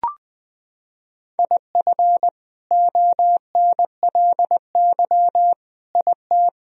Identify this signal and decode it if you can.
Morse code, heard as a keyed tone, about 20 wpm, 700 Hz. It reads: IF ONLY IT
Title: Sentences from Top 500 Words and Abbreviations 20wpm